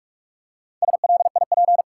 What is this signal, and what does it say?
Morse code, heard as a keyed tone, about 43 wpm, 700 Hz. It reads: SLIP
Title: Four Letter Words - 45wpm